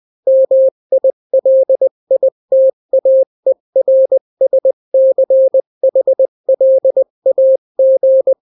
Morse code, heard as a keyed tone, about 20 wpm, 530 Hz. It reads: MILITAERSCHLAG
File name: daily / 18wpm